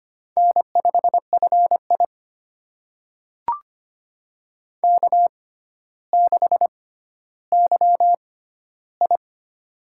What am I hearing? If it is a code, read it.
Morse code, heard as a keyed tone, about 25 wpm, spaced as for 13 wpm, 700 Hz. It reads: N5FI K6YI
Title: USA Call Signs - 4 Characters - 6x Character Spacing - 25wpm